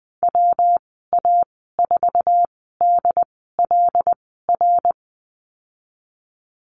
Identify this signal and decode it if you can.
Morse code, heard as a keyed tone, about 20 wpm, spaced as for 15 wpm, 700 Hz. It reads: WA4DLR
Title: 173 Lesson 28g - Number 9 - Call Signs 20wpm